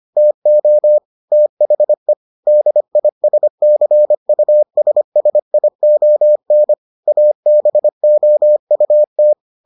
Morse code, heard as a keyed tone, about 25 wpm, 600 Hz. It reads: TO THE DISCUSSION ABOUT